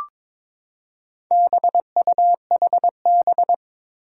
Morse code, heard as a keyed tone, about 22 wpm, 700 Hz. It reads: BUHB
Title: Four Challenging Letter-Number Combinations 22wpm